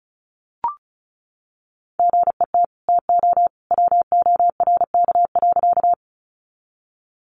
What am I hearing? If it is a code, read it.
Morse code, heard as a keyed tone, about 34 wpm, 700 Hz. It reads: GET TO WORK.
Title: Sentences from Top 200 Words 35wpm